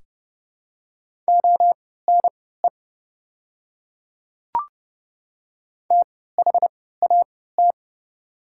Morse code, heard as a keyed tone, about 30 wpm, spaced as for 17 wpm, 700 Hz. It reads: ONE THAT
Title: 3x Character Spacing - Top 100 Words - 30wpm